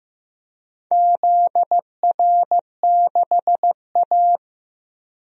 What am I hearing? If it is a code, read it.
Morse code, heard as a keyed tone, about 16 wpm, 700 Hz. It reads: ZR6A
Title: DXCC-100 Call Signs - 4 Characters - Repeated 2x - 15wpm